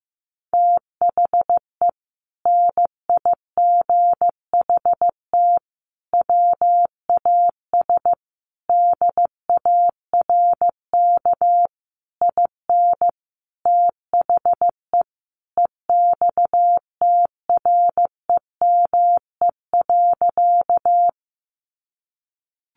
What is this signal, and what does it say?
Morse code, heard as a keyed tone, about 15 wpm, 700 Hz. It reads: THE NIGHT WAS DARK IN THE EXTREME.
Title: Sentences from Top 1300 Words 15wpm